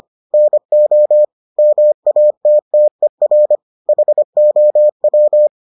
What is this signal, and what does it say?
Morse code, heard as a keyed tone, about 25 wpm, 600 Hz. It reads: NO MATTER HOW